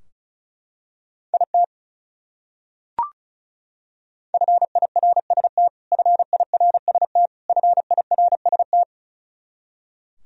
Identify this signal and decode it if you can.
Morse code, heard as a keyed tone, about 36 wpm, 700 Hz. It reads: IT FIRST FIRST FIRST
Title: Top 100 Words Repeated 35wpm